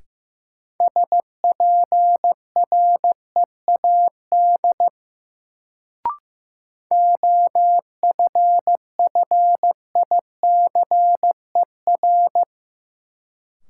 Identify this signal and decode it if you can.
Morse code, heard as a keyed tone, about 15 wpm, 700 Hz. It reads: SPREAD OFFICER
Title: Sentences from Top 1200 Words - New Word Drill - 15wpm